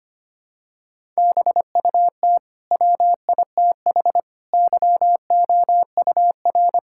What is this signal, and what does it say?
Morse code, heard as a keyed tone, about 25 wpm, 700 Hz. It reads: BUT WITH YOUR
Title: Sentences from Top 1000 Words - Review Part 6 - 25wpm